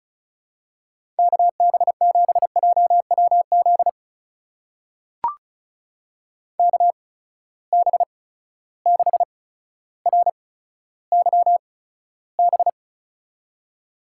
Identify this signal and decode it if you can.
Morse code, heard as a keyed tone, about 36 wpm, spaced as for 15 wpm, 700 Hz. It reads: KB7JWZ KB6RYB
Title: USA Call Signs - 6 Characters - 8x Character Spacing - 35wpm